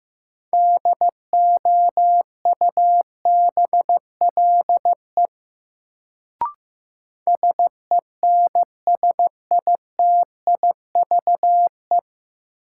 Morse code, heard as a keyed tone, about 15 wpm, 700 Hz. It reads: DOUBLE SENSITIVE
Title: Sentences from Top 1400 Words - New Word Drill Alt - 15wpm